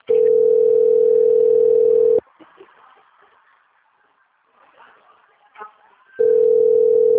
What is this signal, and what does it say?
Ringback tone (North America), 440 + 480 Hz.